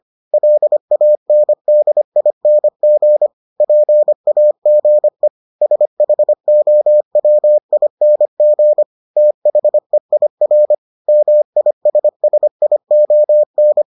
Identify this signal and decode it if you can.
Morse code, heard as a keyed tone, about 25 wpm, 600 Hz. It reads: LANDING PAGE SHOWING THEIR MISSION